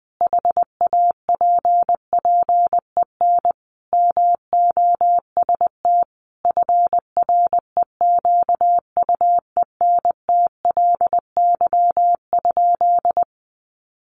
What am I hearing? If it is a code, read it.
Morse code, heard as a keyed tone, about 20 wpm, 700 Hz. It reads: HAPPEN MOST FREQUENTLY?
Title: Sentences from Top 1300 Words 20wpm